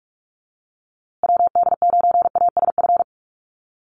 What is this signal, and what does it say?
Morse code, heard as a keyed tone, about 44 wpm, 700 Hz. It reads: WD9ASF
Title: USA Call Signs Repeated 2x 45wpm